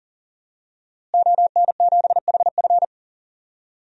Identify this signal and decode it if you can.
Morse code, heard as a keyed tone, about 37 wpm, 700 Hz. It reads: ON7HF
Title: DXCC-100 Call Signs - 5 Characters - Repeated 3x - 40wpm